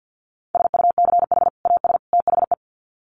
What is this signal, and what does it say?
Morse code, heard as a keyed tone, about 48 wpm, 700 Hz. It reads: SUCH AS THE
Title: Sets of 3 Words 50wpm